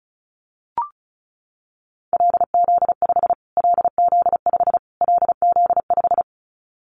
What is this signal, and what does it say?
Morse code, heard as a keyed tone, about 34 wpm, 700 Hz. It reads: LZ5 LZ5 LZ5
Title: Callsign Prefixes for Top 25 DXCC - Repeated 3x - 35wpm